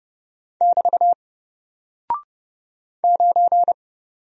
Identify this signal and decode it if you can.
Morse code, heard as a keyed tone, about 30 wpm, 700 Hz. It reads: <BT> 9
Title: Single Character (Letter:Number:Punctuation:Prosign) Alt 30wpm